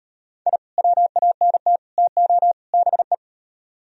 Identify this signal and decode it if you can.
Morse code, heard as a keyed tone, about 37 wpm, 700 Hz. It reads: I WANT TO BE